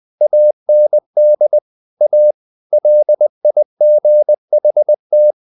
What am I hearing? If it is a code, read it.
Morse code, heard as a keyed tone, about 20 wpm, 600 Hz. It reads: AND A LIGHT
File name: the old man and the sea / the old man and the sea part 1 20wpm